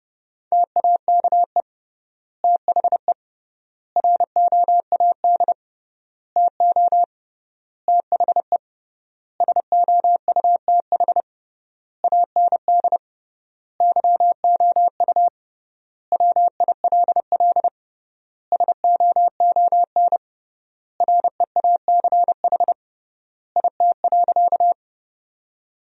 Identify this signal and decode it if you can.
Morse code, heard as a keyed tone, about 30 wpm, 700 Hz. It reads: TAKE THE ROAD TO THE SOUTH AND YOU WILL SOON REACH IT.
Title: 3x Word-Spacing - Sentences from Top 400 Words - 30wpm